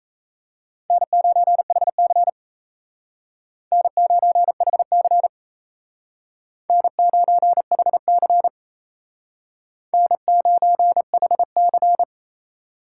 Morse code, heard as a keyed tone, about 36 wpm, 700 Hz. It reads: N9HC N9HC N9HC N9HC